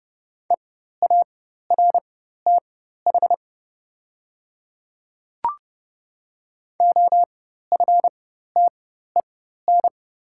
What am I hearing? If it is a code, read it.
Morse code, heard as a keyed tone, about 30 wpm, spaced as for 14 wpm, 700 Hz. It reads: EARTH OFTEN
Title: 4x Character Spacing - Top 300 Words - 30wpm